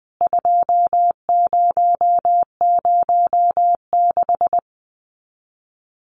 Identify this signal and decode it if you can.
Morse code, heard as a keyed tone, about 20 wpm, 700 Hz. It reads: 2006
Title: CWT Exchange 20wpm